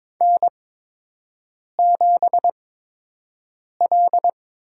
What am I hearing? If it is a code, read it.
Morse code, heard as a keyed tone, about 22 wpm, spaced as for 6 wpm, 700 Hz. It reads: N7L
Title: USA Call Signs - 4 Characters - 8x Character Spacing - 22wpm